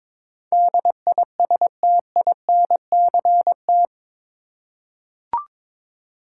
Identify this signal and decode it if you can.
Morse code, heard as a keyed tone, about 21 wpm, 700 Hz. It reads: DISTINCT
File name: practice / 22wpm / Eight Letter Words - 22wpm